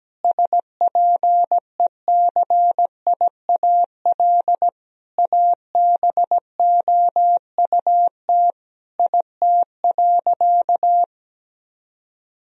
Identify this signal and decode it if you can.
Morse code, heard as a keyed tone, about 17 wpm, 700 Hz. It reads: SPECIAL ABOUT IT.
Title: Sentences from Top 1000 Words - Review Part 3 - 17wpm